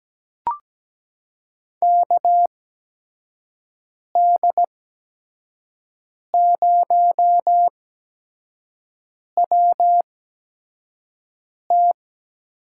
Morse code, heard as a keyed tone, about 17 wpm, spaced as for 5 wpm, 700 Hz. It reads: KD0WT
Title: USA Call Signs - 6 Characters - 8x Character Spacing - 17wpm